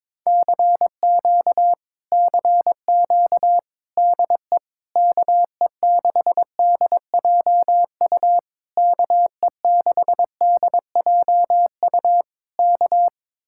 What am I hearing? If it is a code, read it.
Morse code, heard as a keyed tone, about 21 wpm, 700 Hz. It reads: CQ CQ DE KE6DJU KE6DJU K
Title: Speed-Racing Common QSO Phrases - 22wpm